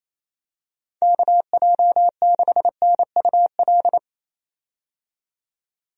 Morse code, heard as a keyed tone, about 28 wpm, 700 Hz. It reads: KJ6NUL
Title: USA Call Signs - 6 Characters - 28wpm